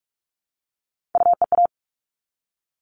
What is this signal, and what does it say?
Morse code, heard as a keyed tone, about 44 wpm, 700 Hz. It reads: UEA
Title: All Cut Numbers - 3-Digit 45wpm